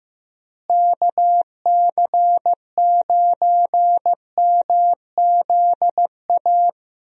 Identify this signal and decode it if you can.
Morse code, heard as a keyed tone, about 15 wpm, 700 Hz. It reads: KC9MZA
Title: USA Call Signs - 6 Characters - Repeated 3x - 15wpm